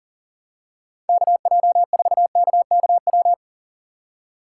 Morse code, heard as a keyed tone, about 40 wpm, 700 Hz. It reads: KJ4KKW
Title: USA Call Signs - 6 Characters - 3x Character Spacing - 40wpm